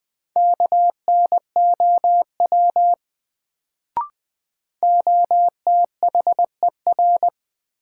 Morse code, heard as a keyed tone, about 20 wpm, 700 Hz. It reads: KNOW OTHER
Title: Sets of 2 Words - New Word Drill - 20wpm